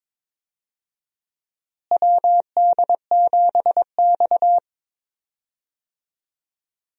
Morse code, heard as a keyed tone, about 22 wpm, 700 Hz. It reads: WD7X
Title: USA Call Signs - 4 Characters - Repeated 2x - 22wpm